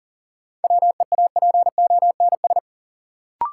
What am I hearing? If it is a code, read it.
Morse code, heard as a keyed tone, about 40 wpm, 700 Hz. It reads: WEAPONS
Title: Sets of 5 Words Encore - New Word Drill - 40wpm